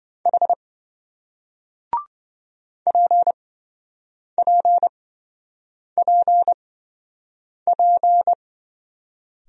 Morse code, heard as a keyed tone, about 27 wpm, spaced as for 7 wpm, 700 Hz. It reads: H PPPP